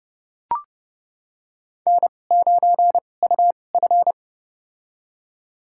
Morse code, heard as a keyed tone, about 30 wpm, spaced as for 22 wpm, 700 Hz. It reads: N9UF
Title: USA Call Signs - 4 Characters - 2x Character Spacing - 30wpm